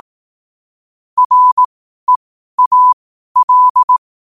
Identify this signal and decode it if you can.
Morse code, heard as a keyed tone, about 18 wpm, spaced as for 12 wpm, 1000 Hz. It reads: REAL
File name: qotdfiles / 12wpm